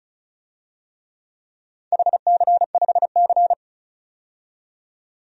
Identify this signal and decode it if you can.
Morse code, heard as a keyed tone, about 36 wpm, 700 Hz. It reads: HC5C
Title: DXCC-100 Call Signs - 4 Characters - 35wpm